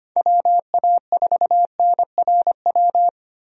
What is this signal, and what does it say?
Morse code, heard as a keyed tone, about 25 wpm, 700 Hz. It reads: WA4NRW
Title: USA Call Signs - 6 Characters - Repeated 3x - 25wpm